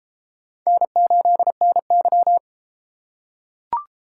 Morse code, heard as a keyed tone, about 30 wpm, 700 Hz. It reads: N8NY